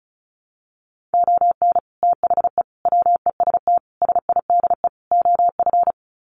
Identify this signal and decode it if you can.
Morse code, heard as a keyed tone, about 34 wpm, 700 Hz. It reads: ON THE WEST SIDE OF
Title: Sets of 5 Words Encore2 35wpm